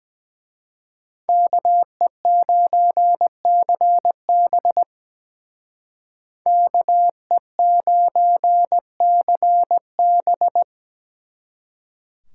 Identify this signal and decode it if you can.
Morse code, heard as a keyed tone, about 20 wpm, 700 Hz. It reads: KE9CB KE9CB
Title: Speed Racing - USA Call Signs 17wpm